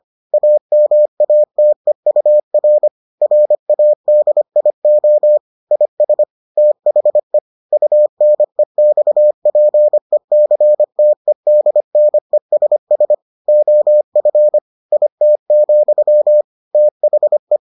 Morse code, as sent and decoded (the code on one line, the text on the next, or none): .- -- .- - . ..- .-. / .-. .- -.. .. --- / .. ... / - .... / ..- -. -..- .--. -.-. - . -.. -. ... ... / --- ..-. / .. - --..-- / - .... .
AMATEUR RADIO IS THE UNEXPECTEDNESS OF IT, THE